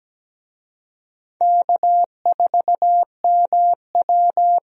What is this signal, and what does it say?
Morse code, heard as a keyed tone, about 17 wpm, 700 Hz. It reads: K4MW
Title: USA Call Signs - 4 Characters - 17wpm